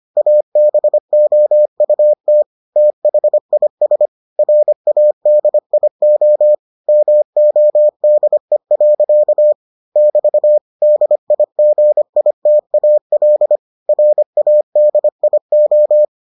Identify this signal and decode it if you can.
Morse code, heard as a keyed tone, about 25 wpm, 600 Hz. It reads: ABOUT THIS RADIO MODE. <BT> DIGITAL RADIO